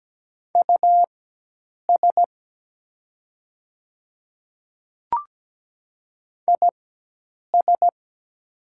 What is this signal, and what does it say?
Morse code, heard as a keyed tone, about 17 wpm, spaced as for 8 wpm, 700 Hz. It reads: US IS
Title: 4x Character Spacing - Top 200 Words - 17wpm